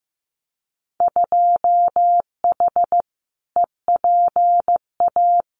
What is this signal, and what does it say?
Morse code, heard as a keyed tone, about 15 wpm, 700 Hz. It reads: 2H EPA
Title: Winter Field Day Exchanges Encore 15wpm